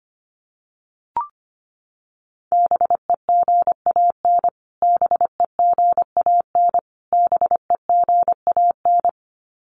Morse code, heard as a keyed tone, about 25 wpm, 700 Hz. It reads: BEGAN BEGAN BEGAN
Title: Top 500 Words Repeated 25wpm